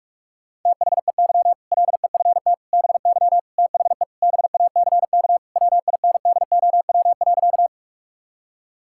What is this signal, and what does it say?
Morse code, heard as a keyed tone, about 46 wpm, 700 Hz. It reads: THEY LEFT BY THE BACK WINDOW.
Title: Sentences from Top 1600 Words 45wpm